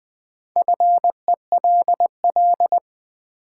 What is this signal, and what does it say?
Morse code, heard as a keyed tone, about 20 wpm, 700 Hz. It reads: FELL
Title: Sentences from Top 1600 Words - New Word Drill - 20wpm